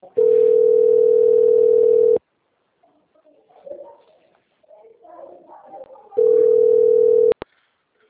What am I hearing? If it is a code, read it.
Ringback tone (North America), 440 + 480 Hz.